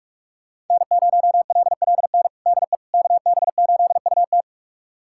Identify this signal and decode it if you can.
Morse code, heard as a keyed tone, about 46 wpm, 700 Hz. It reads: N0LLN DE KB8UT
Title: USA Callsign DE Callsign - 5 Characters - 45wpm